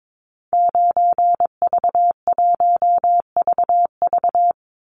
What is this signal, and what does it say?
Morse code, heard as a keyed tone, about 22 wpm, 700 Hz. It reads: 9V1VV
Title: Call Signs Repeated 2x - 22wpm